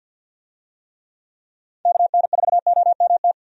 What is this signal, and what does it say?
Morse code, heard as a keyed tone, about 47 wpm, 700 Hz. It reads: KN4OMT
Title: USA Call Signs - 6 Characters - 50wpm